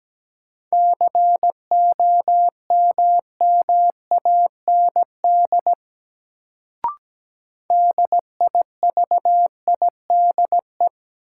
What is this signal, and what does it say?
Morse code, heard as a keyed tone, about 17 wpm, 700 Hz. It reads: COMMAND DIVIDE
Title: Sentences from Top 1600 Words - New Word Drill Alt - 17wpm